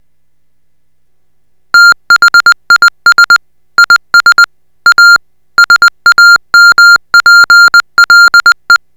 Morse code, heard as a keyed tone, about 20 wpm, 1400 Hz. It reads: THIS IS A SAMPLE